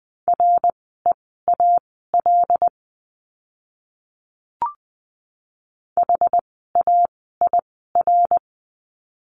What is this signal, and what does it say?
Morse code, heard as a keyed tone, about 20 wpm, spaced as for 14 wpm, 700 Hz. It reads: REAL HAIR
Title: 101 Lesson 17f - Letter u - Words 20wpm